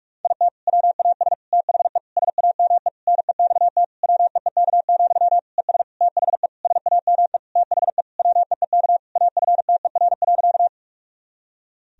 Morse code, heard as a keyed tone, about 41 wpm, 700 Hz. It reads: IT WAS THE SAME NEXT WEEK, ES THE SAME THE WEEK AFTER.
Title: Sentences from Top 500 Words and Abbreviations 45wpm